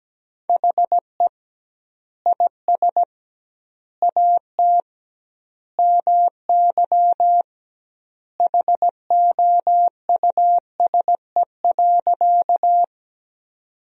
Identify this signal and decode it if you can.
Morse code, heard as a keyed tone, about 17 wpm, 700 Hz. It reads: HE IS AT MY HOUSE.
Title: 2x Word-Spacing - Sentences from Top 200 Words - 17wpm